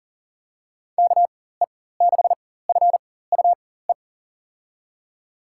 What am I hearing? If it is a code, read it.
Morse code, heard as a keyed tone, about 40 wpm, spaced as for 19 wpm, 700 Hz. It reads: KE6FUE
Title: USA Call Signs - 6 Characters - 4x Character Spacing - 40wpm